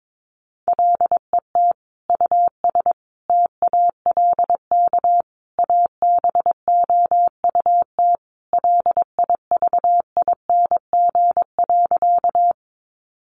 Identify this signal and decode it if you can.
Morse code, heard as a keyed tone, about 22 wpm, 700 Hz. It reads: LET US TALK ABOUT LIVING.
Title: Sentences from Top 1100 Words 22wpm